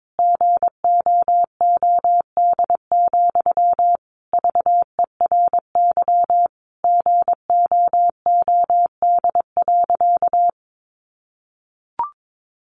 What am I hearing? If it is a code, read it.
Morse code, heard as a keyed tone, about 22 wpm, 700 Hz. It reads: GOOD, VERY GOOD.
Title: Sentences from Top 200 Words 22wpm